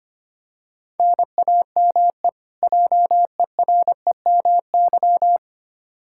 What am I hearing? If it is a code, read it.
Morse code, heard as a keyed tone, about 25 wpm, 700 Hz. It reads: NAME JEREMY
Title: K1USN SST QSO Practice 25wpm